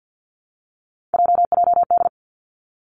Morse code, heard as a keyed tone, about 50 wpm, 700 Hz. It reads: JJD
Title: Three Challenging Letter-Number Combinations 50wpm